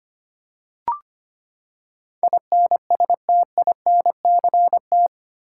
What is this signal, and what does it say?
Morse code, heard as a keyed tone, about 25 wpm, 700 Hz. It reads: INSTINCT
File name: Eight Letter Words - 25wpm